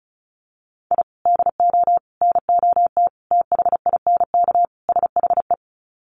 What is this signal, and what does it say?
Morse code, heard as a keyed tone, about 34 wpm, 700 Hz. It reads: I DO NOT THINK SHE
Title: Sets of 5 Words 35wpm